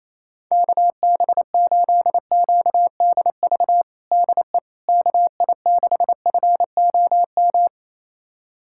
Morse code, heard as a keyed tone, about 27 wpm, 700 Hz. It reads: KB8QDV DE KI6FOM
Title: USA Callsign DE Callsign - 6 Characters - 28wpm